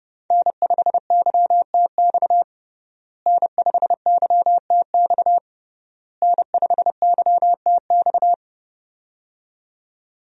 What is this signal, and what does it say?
Morse code, heard as a keyed tone, about 31 wpm, 700 Hz. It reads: N5YTX N5YTX N5YTX
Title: USA Call Signs - 5 Characters - Repeated 3x - 30wpm